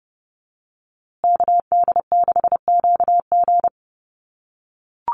Morse code, heard as a keyed tone, about 30 wpm, 700 Hz. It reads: KD6QG
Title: USA Call Signs Repeated 3x 30wpm